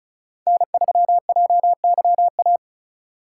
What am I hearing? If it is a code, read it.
Morse code, heard as a keyed tone, about 36 wpm, 700 Hz. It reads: N3JYA
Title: USA Call Signs - 5 Characters - 8x Character Spacing - 35wpm